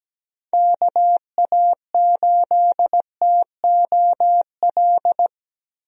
Morse code, heard as a keyed tone, about 17 wpm, 700 Hz. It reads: KA8TOL
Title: USA Call Signs - 6 Characters - Repeated 3x - 17wpm